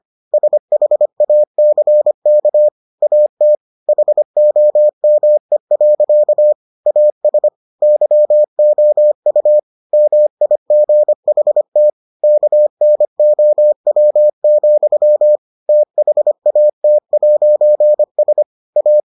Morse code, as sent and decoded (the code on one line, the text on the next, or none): ... .... .- -.-. -.- / .- - / .... --- -- . .-.-.- / .- ... / -.-- --- ..- / -- .. --. .... - / -.- -. --- .-- --..-- / - .... .- - .----. ... / .-
SHACK AT HOME. AS YOU MIGHT KNOW, THAT'S A